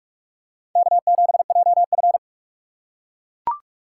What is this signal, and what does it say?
Morse code, heard as a keyed tone, about 41 wpm, 700 Hz. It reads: K7JF
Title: USA Call Signs - 4 Characters - 4x Character Spacing - 45wpm